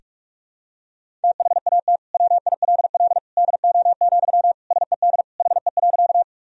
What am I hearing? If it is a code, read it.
Morse code, heard as a keyed tone, about 41 wpm, 700 Hz. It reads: THAT WILL DO, SED HE.
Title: Sentences from Top 100 Words and Abbreviations 45wpm